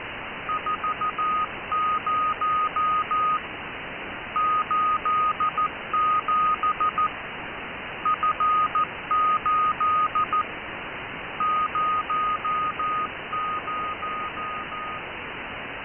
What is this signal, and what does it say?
Morse code, heard as a keyed tone, about 14 wpm, 1240 Hz. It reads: 40 87 F8 00